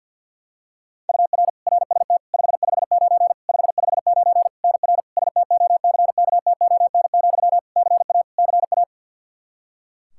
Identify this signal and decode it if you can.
Morse code, heard as a keyed tone, about 50 wpm, 700 Hz. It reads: UR RST 559 559 NR STOCKTON, CA CA
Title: Practice QSO - Exchange 1 - Beginning - 50wpm